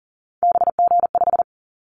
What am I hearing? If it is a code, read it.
Morse code, heard as a keyed tone, about 41 wpm, 700 Hz. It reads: BG5
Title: Callsign Prefixes for Top 100 DXCC - 40wpm